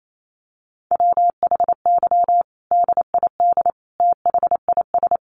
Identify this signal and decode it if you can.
Morse code, heard as a keyed tone, about 28 wpm, 700 Hz. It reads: WHY DID THIS
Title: Sentences from Top 900 Words 28wpm